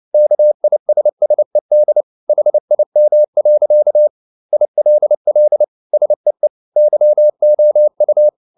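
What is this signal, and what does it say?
Morse code, heard as a keyed tone, about 29 wpm, 600 Hz. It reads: KISSED HIM. ILL SEE YOU
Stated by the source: Morse code at 29 wpm, sending Ken Follett's novel Never in English, from CW med Gnister.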